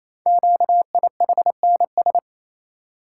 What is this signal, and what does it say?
Morse code, heard as a keyed tone, about 27 wpm, 700 Hz. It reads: QIHNS
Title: Code Groups - 5 Letters - 28wpm